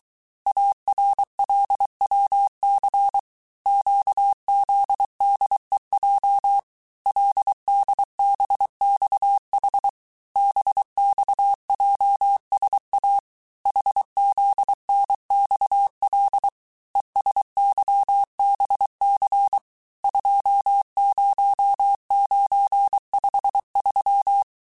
Morse code, heard as a keyed tone, about 23 wpm, 800 Hz. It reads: ARLWC QZDEJ LDBXH BXJSA HZNXL ESYBC 20953
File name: cours-du-mercredi / audios / cours-855-texte-chiffre-vitesse-22-mpm